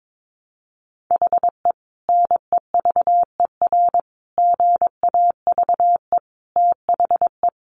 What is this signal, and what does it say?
Morse code, heard as a keyed tone, about 22 wpm, 700 Hz. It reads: HE NEVER GAVE THE
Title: Sentences from Top 1200 Words 22wpm